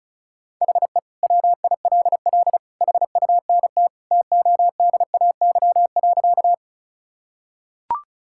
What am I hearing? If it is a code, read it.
Morse code, heard as a keyed tone, about 36 wpm, 700 Hz. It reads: HE WILL HUNT TODAY.
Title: Sentences from Top 1600 Words 35wpm